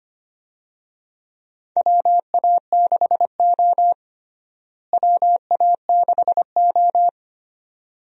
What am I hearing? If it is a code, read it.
Morse code, heard as a keyed tone, about 25 wpm, 700 Hz. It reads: WA6O WA6O
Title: USA Call Signs - 4 Characters - Repeated 2x - 25wpm